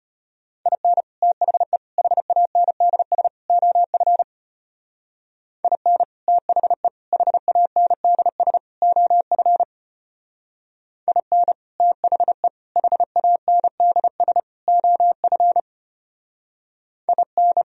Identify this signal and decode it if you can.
Morse code, heard as a keyed tone, about 32 wpm, 700 Hz. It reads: IN THE HANDS OF IN THE HANDS OF IN THE HANDS OF IN